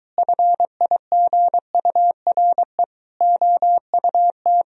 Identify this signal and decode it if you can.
Morse code, heard as a keyed tone, about 23 wpm, 700 Hz. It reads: FIGURE OUT